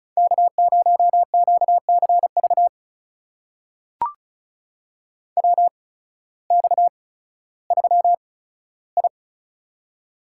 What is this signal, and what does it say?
Morse code, heard as a keyed tone, about 36 wpm, spaced as for 16 wpm, 700 Hz. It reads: K0QCV WX3I
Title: USA Call Signs - 8x Character Spacing - 35wpm